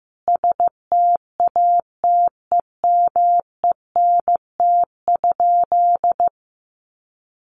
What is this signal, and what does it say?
Morse code, heard as a keyed tone, about 15 wpm, 700 Hz. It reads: STATEMENT?
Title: Sentences from Top 1200 Words 15wpm